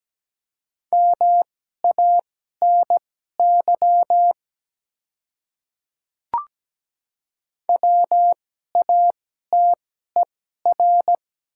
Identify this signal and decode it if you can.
Morse code, heard as a keyed tone, about 17 wpm, spaced as for 12 wpm, 700 Hz. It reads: MANY WATER
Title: 2x Character Spacing - Top 200 Words - 17wpm